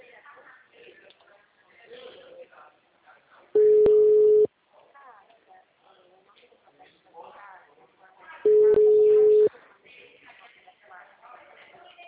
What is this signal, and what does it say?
Ringback tone (Europe, 425 Hz), with speech.